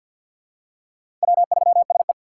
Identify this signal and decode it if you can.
Morse code, heard as a keyed tone, about 46 wpm, 700 Hz. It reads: W3SE
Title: USA Call Signs - 4 Characters - 2x Character Spacing - 50wpm